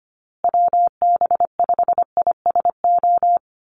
Morse code, heard as a keyed tone, about 25 wpm, 700 Hz. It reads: WB5ISO
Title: USA Call Signs Repeated 3x 25wpm